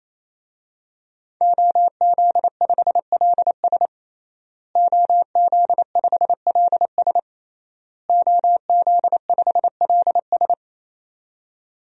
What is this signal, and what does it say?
Morse code, heard as a keyed tone, about 28 wpm, 700 Hz. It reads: OZ5LS OZ5LS OZ5LS
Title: DXCC-100 Call Signs - 5 Characters - Repeated 3x - 28wpm